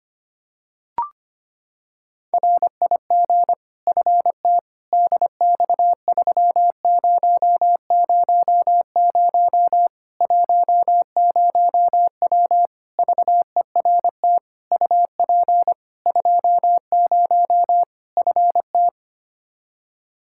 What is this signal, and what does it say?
Morse code, heard as a keyed tone, about 25 wpm, 700 Hz. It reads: RIG FT DX3000 10W VERT UP 20 FT
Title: Practice QSO - Exchange 2 - Info - 25wpm